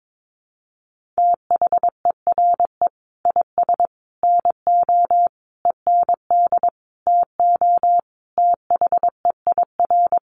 Morse code, heard as a keyed tone, about 22 wpm, 700 Hz. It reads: THERE IS NO END TO THEIR
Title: Sentences from Top 1100 Words 22wpm